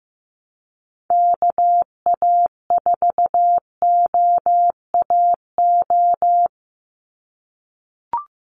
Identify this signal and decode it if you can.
Morse code, heard as a keyed tone, about 15 wpm, 700 Hz. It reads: KA4OAO
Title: USA Call Signs 15wpm